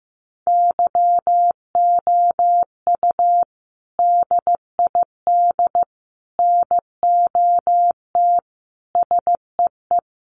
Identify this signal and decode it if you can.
Morse code, heard as a keyed tone, about 15 wpm, 700 Hz. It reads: YOU DID NOT SEE
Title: Sentences from Top 900 Words 15wpm